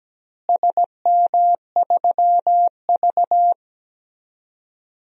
Morse code, heard as a keyed tone, about 17 wpm, 700 Hz. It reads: SM3V
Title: DXCC-100 Call Signs - 4 Characters - Repeated 3x - 17wpm